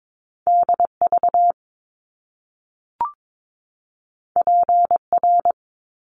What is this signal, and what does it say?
Morse code, heard as a keyed tone, about 22 wpm, 700 Hz. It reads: DV PR
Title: Top Two Letter Combinations in Words 22wpm